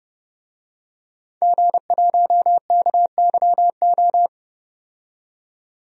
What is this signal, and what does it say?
Morse code, heard as a keyed tone, about 32 wpm, 700 Hz. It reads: G1KYO
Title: DXCC-100 Call Signs - 5 Characters - 30wpm